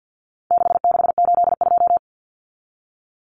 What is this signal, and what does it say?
Morse code, heard as a keyed tone, about 50 wpm, 700 Hz. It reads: 6682
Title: Numbers - 4-Digit 50wpm